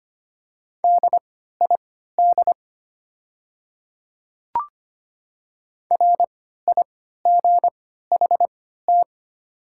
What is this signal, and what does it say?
Morse code, heard as a keyed tone, about 25 wpm, spaced as for 14 wpm, 700 Hz. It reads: DID RIGHT
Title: 3x Character Spacing - Top 300 Words - 25wpm